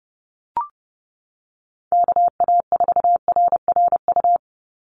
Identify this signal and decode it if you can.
Morse code, heard as a keyed tone, about 30 wpm, 700 Hz. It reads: KA4RRU
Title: Call Signs Repeated 3x - 30wpm